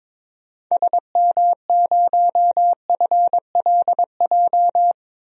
Morse code, heard as a keyed tone, about 22 wpm, 700 Hz. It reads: SM0FLJ
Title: DXCC-100 Call Signs - 6 Characters - 22wpm